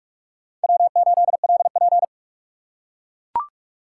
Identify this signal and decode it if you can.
Morse code, heard as a keyed tone, about 39 wpm, 700 Hz. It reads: W8LP